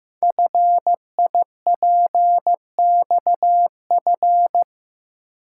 Morse code, heard as a keyed tone, about 15 wpm, 700 Hz. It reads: FIPXF
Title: Code Groups - 5 Letters - 15wpm